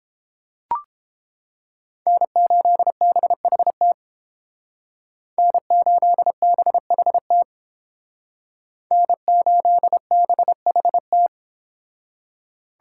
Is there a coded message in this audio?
Yes: Morse code from 2.1 to 11.3 s, heard as a keyed tone, about 30 wpm, 700 Hz. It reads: N8BHT N8BHT N8BHT